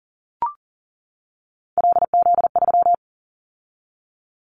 Morse code, heard as a keyed tone, about 40 wpm, 700 Hz. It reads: LZ3
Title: Callsign Prefixes for Top 100 DXCC - 40wpm